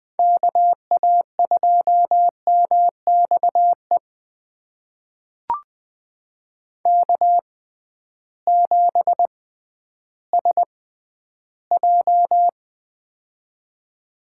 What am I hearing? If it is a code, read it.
Morse code, heard as a keyed tone, about 21 wpm, spaced as for 12 wpm, 700 Hz. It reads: KA2MXE K7SJ
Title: USA Call Signs - 6x Character Spacing - 20wpm